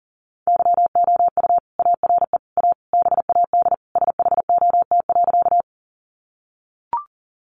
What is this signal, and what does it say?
Morse code, heard as a keyed tone, about 40 wpm, 700 Hz. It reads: YOU ARE A BAD SHOT.
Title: Sentences from Top 1100 Words 40wpm